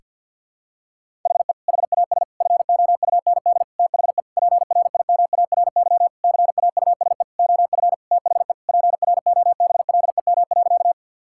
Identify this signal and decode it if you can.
Morse code, heard as a keyed tone, about 52 wpm, 700 Hz. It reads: HE HAS FOUND THE PRIMARY CAUSE OF THE PROBLEM.
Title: Sentences from Top 1000 Words - Review Part 5 - 50wpm